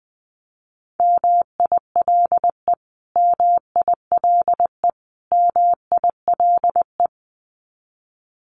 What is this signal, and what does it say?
Morse code, heard as a keyed tone, about 20 wpm, 700 Hz. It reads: MILE MILE MILE
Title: Top 500 Words Repeated 20wpm